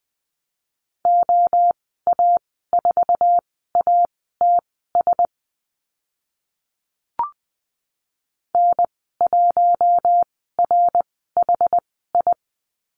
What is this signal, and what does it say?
Morse code, heard as a keyed tone, about 20 wpm, spaced as for 15 wpm, 700 Hz. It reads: OA4ATS N1RHI
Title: 069 Lesson 12g - Letter d - Call Signs 20wpm